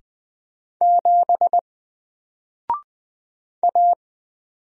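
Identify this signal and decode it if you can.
Morse code, heard as a keyed tone, about 20 wpm, 700 Hz. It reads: 7 A
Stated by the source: Single letter and number Morse drill at 20 wpm, from N4LCD.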